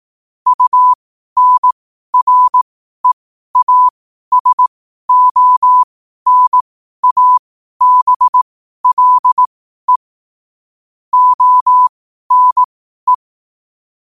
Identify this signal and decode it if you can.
Morse code, heard as a keyed tone, about 18 wpm, spaced as for 12 wpm, 1000 Hz. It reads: UNREASONABLE ONE
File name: qotdfiles / 12wpm